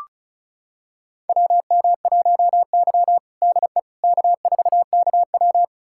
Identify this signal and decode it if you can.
Morse code, heard as a keyed tone, about 36 wpm, 700 Hz. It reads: WM1Y DE K4KW
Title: USA Callsign DE Callsign - 4 Characters - 35wpm